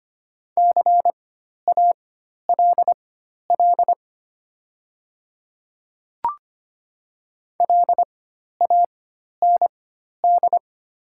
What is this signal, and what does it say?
Morse code, heard as a keyed tone, about 25 wpm, spaced as for 12 wpm, 700 Hz. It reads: CALL LAND
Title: 4x Character Spacing - Top 300 Words - 25wpm